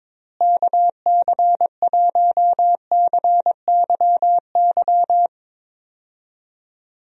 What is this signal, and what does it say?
Morse code, heard as a keyed tone, about 21 wpm, 700 Hz. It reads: KC1CYY
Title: USA Call Signs - 6 Characters - Repeated 2x - 22wpm